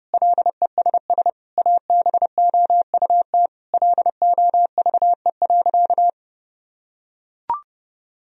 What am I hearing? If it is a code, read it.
Morse code, heard as a keyed tone, about 30 wpm, 700 Hz. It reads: LESS ABOUT LOVE.
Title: Sentences from Top 1000 Words - Review Part 2 - 30wpm